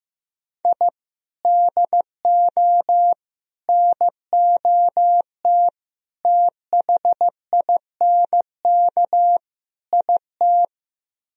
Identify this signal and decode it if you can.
Morse code, heard as a keyed tone, about 17 wpm, 700 Hz. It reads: I DO NOT THINK IT
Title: Speed-Racing - Sets of 5 Words 15wpm